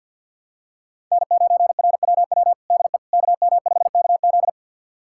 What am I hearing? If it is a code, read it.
Morse code, heard as a keyed tone, about 51 wpm, 700 Hz. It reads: N9RWW DE KM5KZ
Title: USA Callsign DE Callsign - 5 Characters - 50wpm